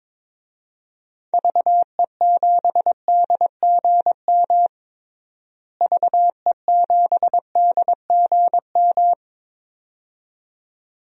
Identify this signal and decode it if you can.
Morse code, heard as a keyed tone, about 22 wpm, 700 Hz. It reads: VE7DGM VE7DGM
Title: DXCC-100 Call Signs - 6 Characters - Repeated 2x - 22wpm